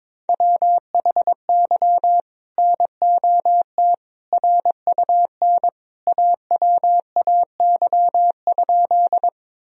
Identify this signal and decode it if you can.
Morse code, heard as a keyed tone, about 22 wpm, 700 Hz. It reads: WHY NOT RUN AWAY?
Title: Sentences from Top 1000 Words - Review Part 1 - 22wpm